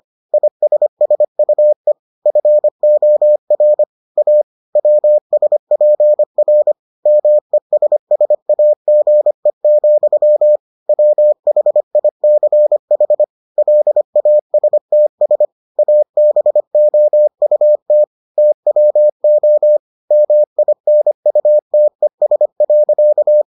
Morse code, heard as a keyed tone, about 25 wpm, 600 Hz. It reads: ISSUE FOR A WSPR MESSAGE, WHICH LASTS ABOUT TWO MINUTES.